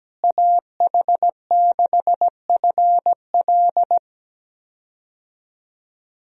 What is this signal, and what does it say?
Morse code, heard as a keyed tone, about 17 wpm, 700 Hz. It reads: AH6FL
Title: USA Call Signs - 5 Characters - Repeated 3x - 17wpm